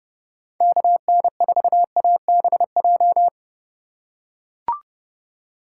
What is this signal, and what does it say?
Morse code, heard as a keyed tone, about 30 wpm, 700 Hz. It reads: KN4ABJ
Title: USA Call Signs - 6 Characters - 4x Character Spacing - 30wpm